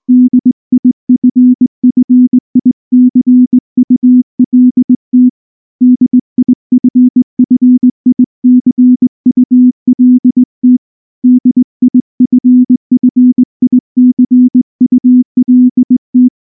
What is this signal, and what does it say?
Morse code, heard as a keyed tone, about 20 wpm, 250 Hz. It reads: DIFFICULT DIFFICULT DIFFICULT
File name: morse / morse-training-20WPM-5